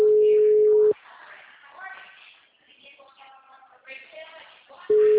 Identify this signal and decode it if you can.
Ringback tone (Europe, 425 Hz), with speech.
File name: out- -2012- - - .1986